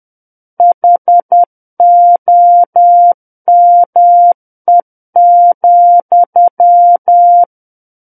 Morse code, heard as a keyed tone, about 10 wpm, 700 Hz. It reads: HOME,
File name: Book One / 01-Chapter 1 - 10 WPM